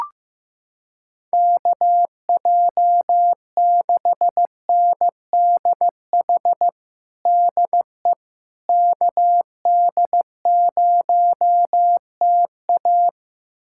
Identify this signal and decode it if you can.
Morse code, heard as a keyed tone, about 15 wpm, 700 Hz. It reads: KJ6NDH DE KD0TA
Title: USA Callsign DE Callsign 15wpm